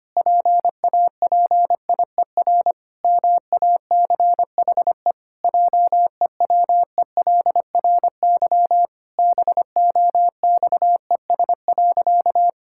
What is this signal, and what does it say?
Morse code, heard as a keyed tone, about 25 wpm, 700 Hz. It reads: PAPIER MACHE JEWELRY BOXES.
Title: Pangram Challenge v2 - 25wpm